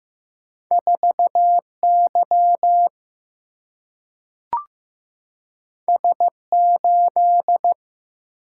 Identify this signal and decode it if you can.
Morse code, heard as a keyed tone, about 15 wpm, 700 Hz. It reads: 4Y S8
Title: Single-Letter Single-Number 15wpm